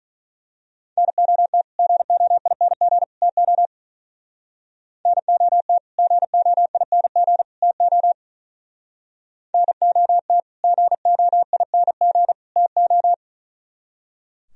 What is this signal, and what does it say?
Morse code, heard as a keyed tone, about 41 wpm, 690 Hz. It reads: NOT GOING TO NOT GOING TO NOT GOING TO
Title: Speed-Racing - Sets of 3 Words 35wpm